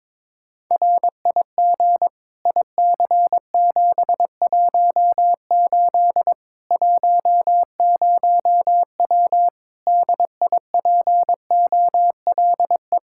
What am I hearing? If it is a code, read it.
Morse code, heard as a keyed tone, about 22 wpm, 700 Hz. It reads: RIG IC718 10W DIPOLE
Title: Practice QSO - Exchange 2 - Info - 22wpm